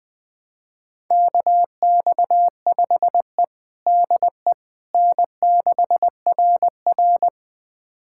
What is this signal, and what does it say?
Morse code, heard as a keyed tone, about 21 wpm, 700 Hz. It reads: KX5E DE N6RR
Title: USA Callsign DE Callsign - 4 Characters - 20wpm